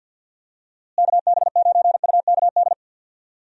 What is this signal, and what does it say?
Morse code, heard as a keyed tone, about 53 wpm, 700 Hz. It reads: KB9UKD
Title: USA Call Signs - 6 Characters - Repeated 3x - 50wpm